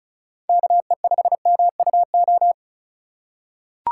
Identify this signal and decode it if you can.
Morse code, heard as a keyed tone, about 36 wpm, 700 Hz. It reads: KE5MUO
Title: USA Call Signs - 6 Characters - 4x Character Spacing - 35wpm